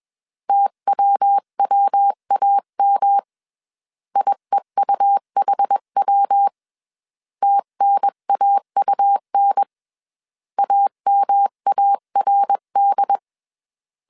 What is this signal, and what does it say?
Morse code, heard as a keyed tone, about 21 wpm, 800 Hz. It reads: TWWAM IEUHW TNAUN AMARD
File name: 20wpm